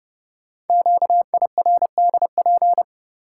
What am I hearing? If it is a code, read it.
Morse code, heard as a keyed tone, about 30 wpm, 700 Hz. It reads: QIRDP
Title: Code Groups - 5 Letter:Numbers - 30wpm